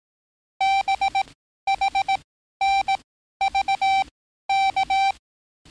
Morse code, heard as a keyed tone, about 18 wpm, spaced as for 12 wpm, 780 Hz. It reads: BHNVK
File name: training / NAVPERS 11049to11060 / NAVPERS 11054 RA RB